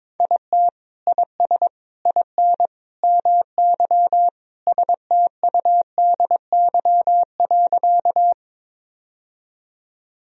Morse code, heard as a keyed tone, about 22 wpm, 700 Hz. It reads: IT IS IN MY STUDY.
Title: Sentences from Top 1000 Words - Review Part 1 - 22wpm